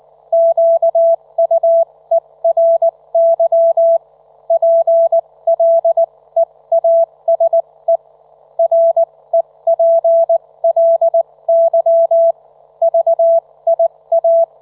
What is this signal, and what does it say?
Morse code, heard as a keyed tone, about 19 wpm, spaced as for 16 wpm, 670 Hz. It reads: QUERY PLEASE REPLY VIA
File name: Trivia